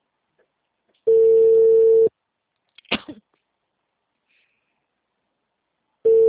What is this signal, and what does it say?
Ringback tone (China, 450 Hz).